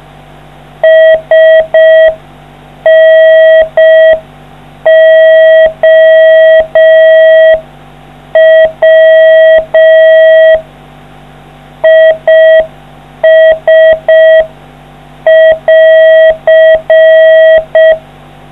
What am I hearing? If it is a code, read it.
Morse code, heard as a keyed tone, about 5 wpm, 640 Hz. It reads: SNOW IS<AR>